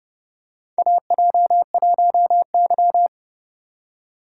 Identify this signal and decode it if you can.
Morse code, heard as a keyed tone, about 30 wpm, 700 Hz. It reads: AJ1Y
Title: USA Call Signs - 4 Characters - 4x Character Spacing - 30wpm